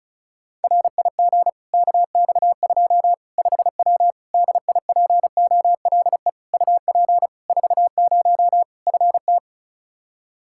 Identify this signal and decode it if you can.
Morse code, heard as a keyed tone, about 34 wpm, 700 Hz. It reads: RIG KX2 5W DIPOLE UP 40 FT
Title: Practice QSO - Exchange 2 - Info - 35wpm